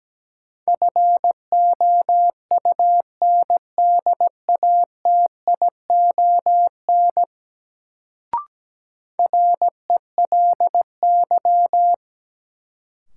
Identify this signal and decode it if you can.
Morse code, heard as a keyed tone, about 17 wpm, 700 Hz. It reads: FOUNDATION RELY
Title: Sentences from Top 1200 Words - New Word Drill Alt - 17wpm